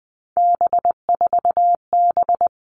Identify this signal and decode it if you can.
Morse code, heard as a keyed tone, about 20 wpm, 700 Hz. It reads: B4B
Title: All Cut Numbers - 3-Digit 20wpm